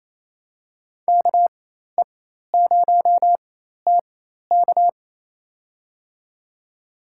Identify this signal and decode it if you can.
Morse code, heard as a keyed tone, about 27 wpm, spaced as for 13 wpm, 700 Hz. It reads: KE0TK
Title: USA Call Signs - 4x Character Spacing - 28wpm